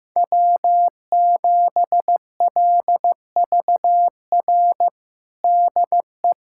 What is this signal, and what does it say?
Morse code, heard as a keyed tone, about 15 wpm, 700 Hz. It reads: W7LVR DE
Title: USA Callsign DE Callsign - 5 Characters - 15wpm